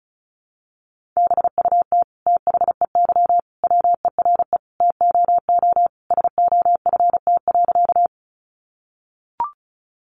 Morse code, heard as a keyed tone, about 34 wpm, 700 Hz. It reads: BUT THEY WERE TOO SOFT.
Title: Sentences from Top 1500 Words 35wpm